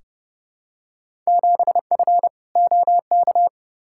Morse code, heard as a keyed tone, about 30 wpm, 700 Hz. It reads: 7F OK
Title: Field Day Exchanges Encore - 30wpm